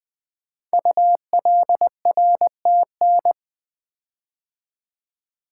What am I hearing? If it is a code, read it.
Morse code, heard as a keyed tone, about 20 wpm, 700 Hz. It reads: ULRTN
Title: Code Groups - 5 Letters - 20wpm